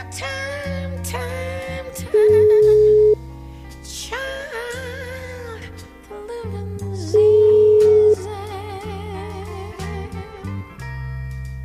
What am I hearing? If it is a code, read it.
Ringback tone (Europe, 425 Hz).